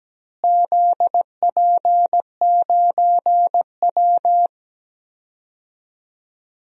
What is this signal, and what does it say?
Morse code, heard as a keyed tone, about 17 wpm, 700 Hz. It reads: ZP9W
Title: DXCC-100 Call Signs - 4 Characters - Repeated 3x - 17wpm